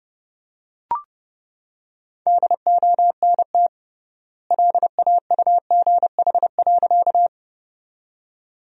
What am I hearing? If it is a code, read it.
Morse code, heard as a keyed tone, about 32 wpm, 700 Hz. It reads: DONT LAUGH.
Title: 3x Word-Spacing - Sentences from Top 500 Words - 30wpm